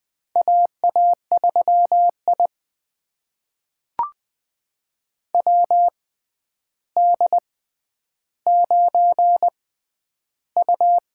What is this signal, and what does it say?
Morse code, heard as a keyed tone, about 21 wpm, spaced as for 10 wpm, 700 Hz. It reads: AA3I WD9U
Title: USA Call Signs - 4 Characters - 6x Character Spacing - 20wpm